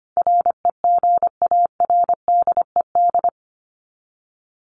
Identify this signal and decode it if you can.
Morse code, heard as a keyed tone, about 25 wpm, 700 Hz. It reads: REGARDED
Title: ED Suffix - Top 100 Words with Base Words 25wpm